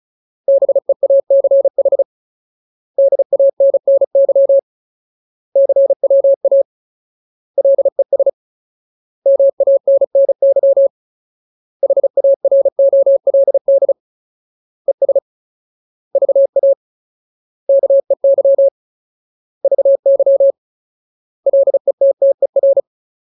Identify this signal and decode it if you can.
Morse code, heard as a keyed tone, about 35 wpm, 550 Hz. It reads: BEACH DANNY CWA LES MANNY HAROLD ES VA KEY VY LETTER
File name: Ultimate Word List / 35WPM Ultimate